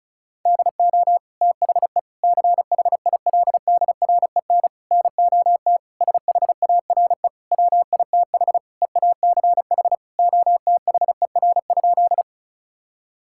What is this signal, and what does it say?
Morse code, heard as a keyed tone, about 34 wpm, 700 Hz. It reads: DO THE CHILDREN NOT SHARE WITH EACH OTHER?
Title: Sentences from Top 1000 Words - Review Part 6 - 35wpm